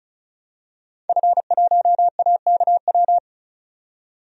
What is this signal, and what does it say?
Morse code, heard as a keyed tone, about 34 wpm, 700 Hz. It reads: F1AKW
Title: DXCC-100 Call Signs - 5 Characters - 35wpm